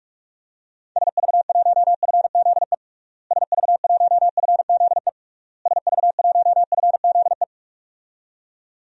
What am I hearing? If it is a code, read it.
Morse code, heard as a keyed tone, about 46 wpm, 700 Hz. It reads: SV1FZE SV1FZE SV1FZE
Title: DXCC-100 Call Signs - 6 Characters - Repeated 3x - 45wpm